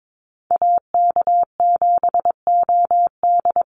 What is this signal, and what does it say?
Morse code, heard as a keyed tone, about 22 wpm, 700 Hz. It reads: AK7OD
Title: USA Call Signs Repeated 3x 22wpm